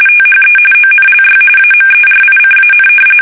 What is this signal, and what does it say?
RTTY, about 45.45 baud. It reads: THE ONLY AMERICAN
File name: fsk850 / battleshipnj